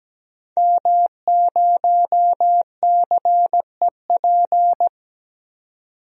Morse code, heard as a keyed tone, about 17 wpm, 700 Hz. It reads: M0CEP
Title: DXCC-100 Call Signs - 5 Characters - 17wpm